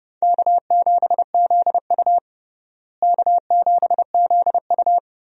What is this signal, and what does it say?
Morse code, heard as a keyed tone, about 32 wpm, 700 Hz. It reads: K7ZU K7ZU
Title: USA Call Signs - 4 Characters - Repeated 2x - 30wpm